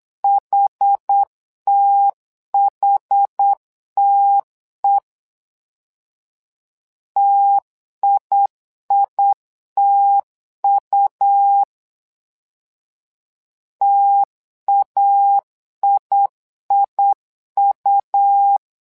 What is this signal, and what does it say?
Morse code, heard as a keyed tone, about 8 wpm, 800 Hz. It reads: HTHTE TIITU TAIIU
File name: LEZIONI / 2 UHDN / 2 UH 8wpm 2129